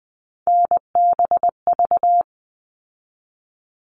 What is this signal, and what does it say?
Morse code, heard as a keyed tone, about 20 wpm, 700 Hz. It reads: NBV
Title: All Cut Numbers - 3-Digit 20wpm